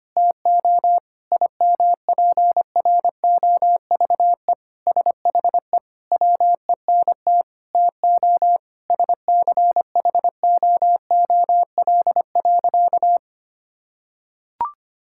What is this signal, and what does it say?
Morse code, heard as a keyed tone, about 25 wpm, 700 Hz. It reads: TO IMPROVE SHE WENT TO SCHOOL.
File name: practice / 25wpm / Sentences with 4-Word Phrases - 25wpm